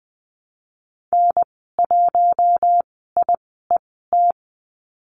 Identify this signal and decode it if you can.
Morse code, heard as a keyed tone, about 20 wpm, spaced as for 14 wpm, 700 Hz. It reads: N1IET
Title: 042 Lesson 8g - Number 1 - Call Signs 20wpm